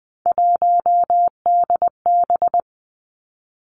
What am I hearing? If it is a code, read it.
Morse code, heard as a keyed tone, about 20 wpm, 700 Hz. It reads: 1DB
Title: Three Challenging Letter-Number Combinations 20wpm